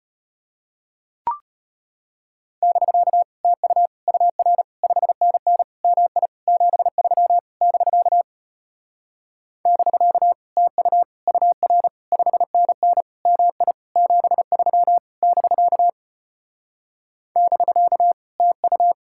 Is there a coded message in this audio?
Yes: Morse code from 2.6 to 18.9 s, heard as a keyed tone, about 34 wpm, 700 Hz. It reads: <BK> TU UR 5NN MI 73 <BK> <BK> TU UR 5NN MI 73 <BK> <BK> TU